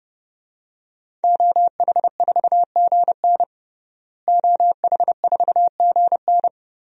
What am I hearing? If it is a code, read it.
Morse code, heard as a keyed tone, about 32 wpm, 700 Hz. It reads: OH4GN OH4GN
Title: DXCC-100 Call Signs - 5 Characters - Repeated 3x - 30wpm